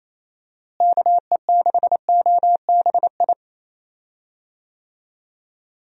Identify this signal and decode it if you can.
Morse code, heard as a keyed tone, about 28 wpm, 700 Hz. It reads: KE6OBI
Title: USA Call Signs - 6 Characters - Repeated 3x - 28wpm